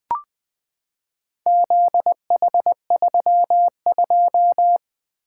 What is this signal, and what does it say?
Morse code, heard as a keyed tone, about 20 wpm, 700 Hz. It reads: ZH32
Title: Four Letter-Number Combinations - 20wpm